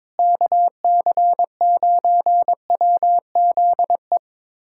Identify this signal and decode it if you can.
Morse code, heard as a keyed tone, about 22 wpm, 700 Hz. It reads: KC9WZE
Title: USA Callsign DE Callsign - 6 Characters - 22wpm